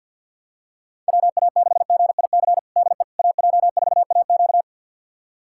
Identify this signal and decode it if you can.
Morse code, heard as a keyed tone, about 52 wpm, 700 Hz. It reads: WA6GIC DE AJ4AQ
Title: USA Callsign DE Callsign 50wpm